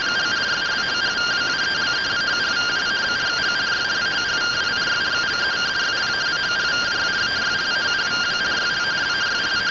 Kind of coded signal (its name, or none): two-tone teleprinter signal